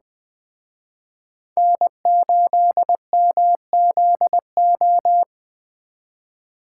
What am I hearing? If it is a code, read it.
Morse code, heard as a keyed tone, about 19 wpm, 700 Hz. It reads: N8MZO